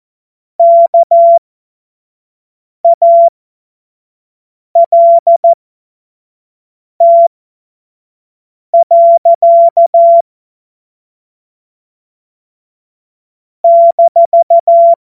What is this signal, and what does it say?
Morse code, heard as a keyed tone, about 14 wpm, spaced as for 5 wpm, 680 Hz. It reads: KALT. -
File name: Morsekurs / Witze / Witz 03. Mai 2022 25 F70